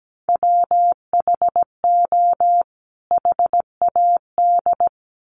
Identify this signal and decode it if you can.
Morse code, heard as a keyed tone, about 17 wpm, 700 Hz. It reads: WHO HAD
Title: Sets of 2 Words 17wpm